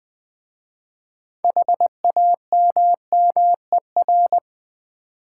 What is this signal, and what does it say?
Morse code, heard as a keyed tone, about 20 wpm, 700 Hz. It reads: HAMMER
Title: Easy Words with Hints - 20wpm